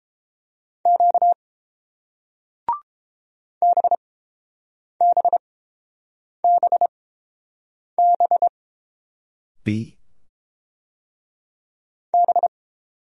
Morse code, heard as a keyed tone, about 30 wpm, spaced as for 7 wpm, 700 Hz. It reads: Q BBBB B